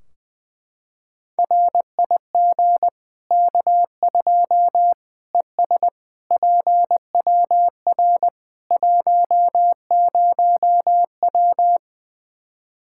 Morse code, heard as a keyed tone, about 19 wpm, 700 Hz. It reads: RIG K2 ES PWR 10W
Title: Practice QSO - Exchange 2 - Info - 20wpm